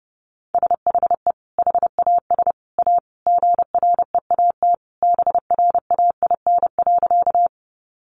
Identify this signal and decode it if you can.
Morse code, heard as a keyed tone, about 30 wpm, 700 Hz. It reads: SHE HAS A GREAT BRAIN.
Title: Sentences from Top 1100 Words 30wpm